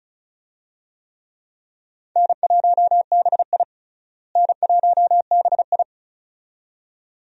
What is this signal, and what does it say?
Morse code, heard as a keyed tone, about 34 wpm, 700 Hz. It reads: N1BI N1BI
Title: USA Call Signs - 4 Characters - Repeated 2x - 35wpm